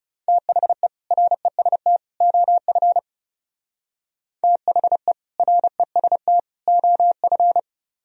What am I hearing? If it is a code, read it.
Morse code, heard as a keyed tone, about 37 wpm, 700 Hz. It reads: THE REST OF THE REST OF
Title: Speed-Racing - Sets of 3 Words 30wpm